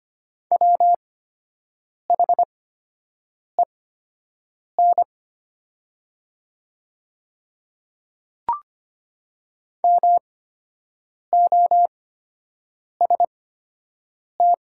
Morse code, heard as a keyed tone, about 25 wpm, spaced as for 7 wpm, 700 Hz. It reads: WHEN MOST
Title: 8x Character Spacing - Top 100 Words - 25wpm